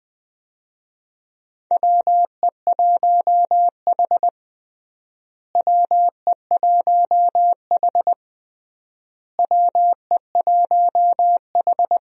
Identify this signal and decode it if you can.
Morse code, heard as a keyed tone, about 20 wpm, 700 Hz. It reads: WE1H WE1H WE1H
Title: USA Call Signs - 4 Characters - Repeated 3x - 20wpm